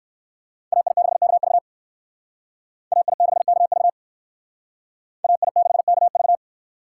Morse code, heard as a keyed tone, about 58 wpm, 700 Hz. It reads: AI6KV AI6KV AI6KV